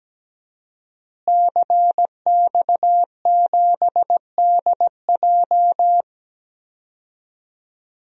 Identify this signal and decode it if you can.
Morse code, heard as a keyed tone, about 17 wpm, 700 Hz. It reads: CX7DJ
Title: DXCC-100 Call Signs - 5 Characters - Repeated 3x - 17wpm